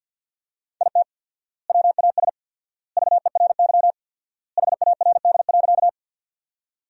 Morse code, heard as a keyed tone, about 51 wpm, 700 Hz. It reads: IT WAS VERY HARD.
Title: 4x Word-Spacing - Sentences from Top 300 Words - 50wpm